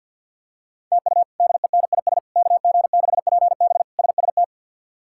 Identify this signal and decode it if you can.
Morse code, heard as a keyed tone, about 50 wpm, 700 Hz. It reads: TU DENIS KG6PB SST
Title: K1USN SST QSO Practice - 50wpm